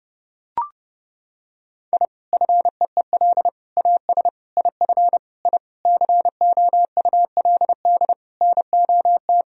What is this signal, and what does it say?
Morse code, heard as a keyed tone, about 30 wpm, 700 Hz. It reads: I FEEL AS IF I COULD NOT
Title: Sentences from Top 1000 Words - Review Part 3 - 30wpm